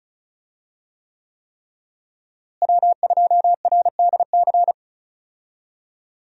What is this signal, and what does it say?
Morse code, heard as a keyed tone, about 34 wpm, 700 Hz. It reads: W2RDC
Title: USA Call Signs - 5 Characters - Repeated 3x - 35wpm